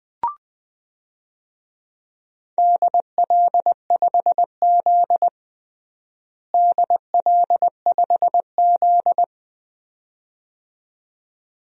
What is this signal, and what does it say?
Morse code, heard as a keyed tone, about 20 wpm, 700 Hz. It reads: DL5Z DL5Z
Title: DXCC-100 Call Signs - 4 Characters - Repeated 2x - 20wpm